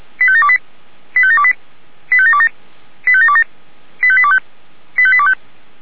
Five-tone selective call (ZVEI), heard as quick runs of five tones, tones of about 70 ms. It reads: 85728 85728 85727 85727 85825 85825